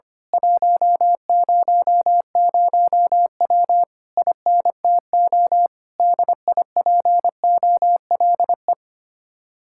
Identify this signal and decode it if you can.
Morse code, heard as a keyed tone, about 24 wpm, 700 Hz. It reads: 100W INTO DIPOLE
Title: Speed-Racing Common QSO Phrases - 25wpm